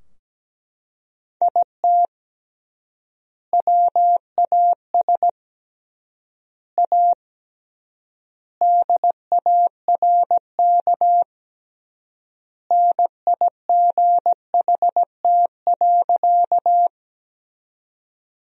Morse code, heard as a keyed tone, about 17 wpm, 700 Hz. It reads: IT WAS A DARK NIGHT.
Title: 3x Word-Spacing - Sentences from Top 500 Words - 17wpm